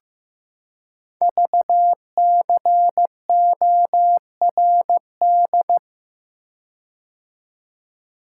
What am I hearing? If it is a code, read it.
Morse code, heard as a keyed tone, about 15 wpm, 700 Hz. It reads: VCORD
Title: Code Groups - 5 Letters - 15wpm